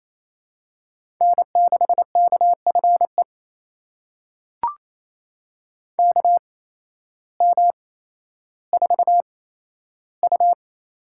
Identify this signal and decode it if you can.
Morse code, heard as a keyed tone, about 28 wpm, spaced as for 13 wpm, 700 Hz. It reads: N6KFE KM4U
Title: USA Call Signs - 8x Character Spacing - 28wpm